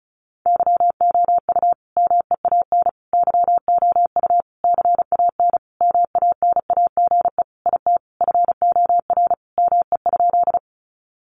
Morse code, heard as a keyed tone, about 34 wpm, 700 Hz. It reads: YOU MEAN YOU CAN MANAGE IT FOR ME?
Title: Sentences from Top 1400 Words 35wpm